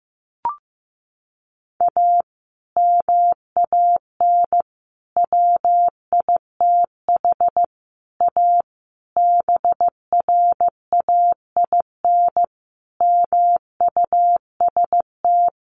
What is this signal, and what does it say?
Morse code, heard as a keyed tone, about 15 wpm, 700 Hz. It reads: A MAN WITH A BRAIN MUST
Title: Sentences from Top 1100 Words 15wpm